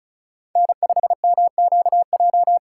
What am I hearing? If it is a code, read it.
Morse code, heard as a keyed tone, about 36 wpm, 700 Hz. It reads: N5MQJ
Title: USA Call Signs - 5 Characters - 6x Character Spacing - 35wpm